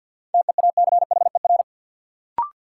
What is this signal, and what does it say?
Morse code, heard as a keyed tone, about 52 wpm, 700 Hz. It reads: TEACHER
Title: Sentences from Top 900 Words - New Word Drill - 50wpm